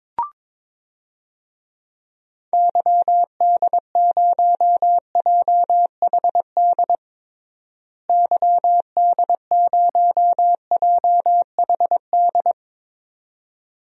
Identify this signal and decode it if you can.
Morse code, heard as a keyed tone, about 22 wpm, 700 Hz. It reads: YD0JHD YD0JHD
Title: DXCC-100 Call Signs - 6 Characters - Repeated 2x - 22wpm